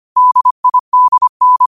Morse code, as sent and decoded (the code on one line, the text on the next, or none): -.. .. -.. -.
DIDN